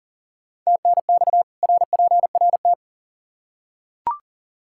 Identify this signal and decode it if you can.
Morse code, heard as a keyed tone, about 40 wpm, 700 Hz. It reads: TNX RPRT
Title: Practice QSO - Exchange 2 - Info - 40wpm